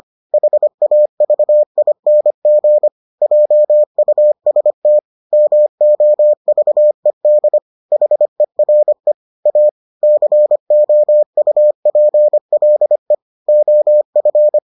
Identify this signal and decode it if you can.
Morse code, heard as a keyed tone, about 25 wpm, 600 Hz. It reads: HAVING JUST MOVED HERE A COUPLE OF